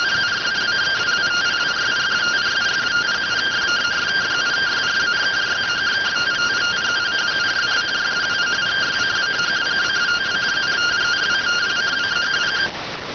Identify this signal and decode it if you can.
Two-tone teleprinter signal, about 170 Hz shift.